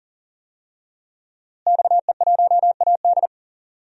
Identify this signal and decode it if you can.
Morse code, heard as a keyed tone, about 40 wpm, 700 Hz. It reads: XE1AD
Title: DXCC-100 Call Signs - 5 Characters - Repeated 2x - 40wpm